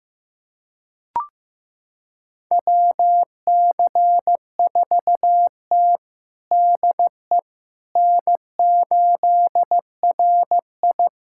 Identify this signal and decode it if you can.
Morse code, heard as a keyed tone, about 16 wpm, 700 Hz. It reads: WC4T DE N8RI
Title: USA Callsign DE Callsign - 4 Characters - 15wpm